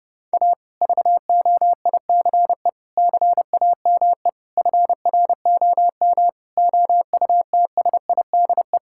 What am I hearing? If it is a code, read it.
Morse code, heard as a keyed tone, about 30 wpm, 700 Hz. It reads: A VOICE CAME FROM OUTSIDE
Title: Sentences from Top 1000 Words - Review Part 6 - 30wpm